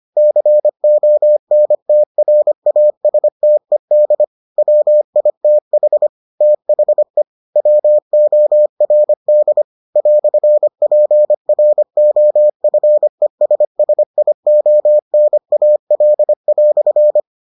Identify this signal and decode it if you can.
Morse code, heard as a keyed tone, about 25 wpm, 600 Hz. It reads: CONTRASTED WITH THE WORD "PROFESSIONAL"